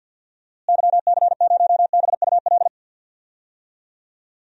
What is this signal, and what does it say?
Morse code, heard as a keyed tone, about 50 wpm, 700 Hz. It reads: YC0BUL
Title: DXCC-100 Call Signs - 6 Characters - Repeated 3x - 50wpm